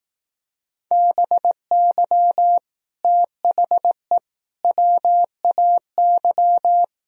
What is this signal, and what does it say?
Morse code, heard as a keyed tone, about 18 wpm, 700 Hz. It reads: BY THE WAY
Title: Speed-Racing - Sets of 3 Words 15wpm